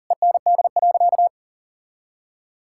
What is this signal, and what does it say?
Morse code, heard as a keyed tone, about 40 wpm, 700 Hz. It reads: END.
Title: Sentences from Top 1600 Words 40wpm